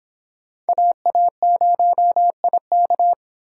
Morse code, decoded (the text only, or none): AA0IK